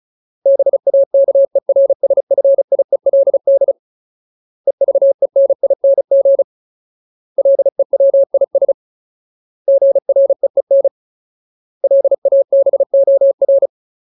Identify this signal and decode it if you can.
Morse code, heard as a keyed tone, about 34 wpm, 550 Hz. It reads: BAKERSFIELD EVENING LEWIS GREEN LABOR
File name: Ultimate Word List / 35WPM Ultimate